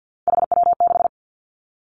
Morse code, heard as a keyed tone, about 50 wpm, 700 Hz. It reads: HW6
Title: Three Challenging Letter-Number Combinations 50wpm